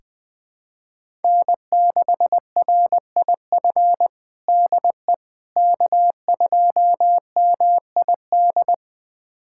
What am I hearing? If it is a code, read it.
Morse code, heard as a keyed tone, about 20 wpm, 700 Hz. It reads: N6RIF DE K2MID
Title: USA Callsign DE Callsign - 5 Characters - 20wpm